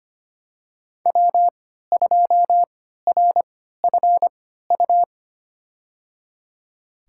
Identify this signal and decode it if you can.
Morse code, heard as a keyed tone, about 25 wpm, spaced as for 14 wpm, 700 Hz. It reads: W2RFU
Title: USA Call Signs - 5 Characters - 3x Character Spacing - 25wpm